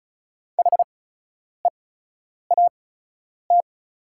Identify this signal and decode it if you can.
Morse code, heard as a keyed tone, about 36 wpm, spaced as for 10 wpm, 700 Hz. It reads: HEAT
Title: 8x Character Spacing - Top 500 Words - 35wpm